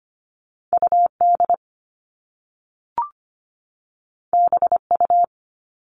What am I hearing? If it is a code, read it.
Morse code, heard as a keyed tone, about 25 wpm, 700 Hz. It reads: UD BU
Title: All Cut Numbers - 2-Digit 25wpm